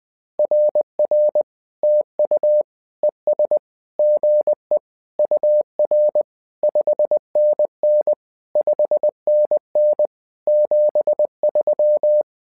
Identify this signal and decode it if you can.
Morse code, heard as a keyed tone, about 20 wpm, 600 Hz. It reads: RR TU ES GE UR 5NN 5NN 73
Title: SOTA QSO Practice 20wpm